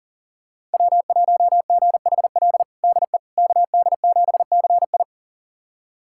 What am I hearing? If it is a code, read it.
Morse code, heard as a keyed tone, about 37 wpm, 700 Hz. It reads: W1GHL DE KD7CI
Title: USA Callsign DE Callsign - 5 Characters - 40wpm